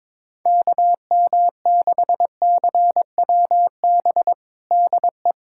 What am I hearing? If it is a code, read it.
Morse code, heard as a keyed tone, about 22 wpm, 700 Hz. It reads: KM6CWB DE
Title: USA Callsign DE Callsign 22wpm